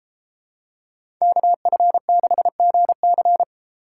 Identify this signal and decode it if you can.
Morse code, heard as a keyed tone, about 30 wpm, 700 Hz. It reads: KF6GC